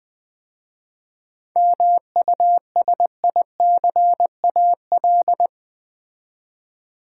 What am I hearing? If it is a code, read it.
Morse code, heard as a keyed tone, about 20 wpm, 700 Hz. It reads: MUSICAL
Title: 7 Letter Words - First 3 Letters Given - 20wpm